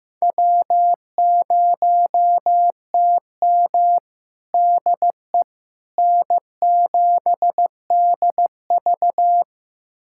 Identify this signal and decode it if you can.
Morse code, heard as a keyed tone, about 15 wpm, 700 Hz. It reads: W0TM DE N7DV
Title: USA Callsign DE Callsign - 4 Characters - 15wpm